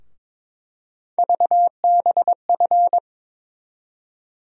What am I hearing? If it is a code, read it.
Morse code, heard as a keyed tone, about 22 wpm, 700 Hz. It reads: VBF
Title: USA Call Sign Postfixes - 3 Letters 22wpm